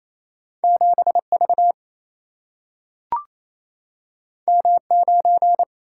Morse code, heard as a keyed tone, about 28 wpm, 700 Hz. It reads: 7V M9
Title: Single-Letter Single-Number 28wpm